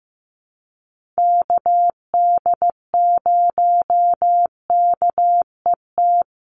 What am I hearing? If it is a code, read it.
Morse code, heard as a keyed tone, about 15 wpm, 700 Hz. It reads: KD0KET
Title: USA Call Signs 15wpm